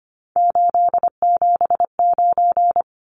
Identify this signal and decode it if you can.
Morse code, heard as a keyed tone, about 25 wpm, 700 Hz. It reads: 879
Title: Numbers - 3-Digit 25wpm